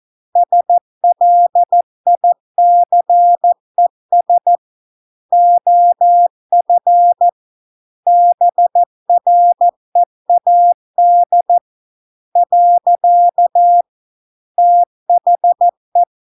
Morse code, heard as a keyed tone, about 14 wpm, 700 Hz. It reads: SLICES OF BREAD . THE